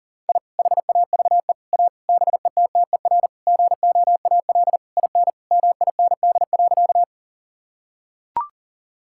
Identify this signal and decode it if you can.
Morse code, heard as a keyed tone, about 40 wpm, 700 Hz. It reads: I HAVE A BETTER GOAL IN MIND.
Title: Sentences from Top 1000 Words - Review Part 5 - 40wpm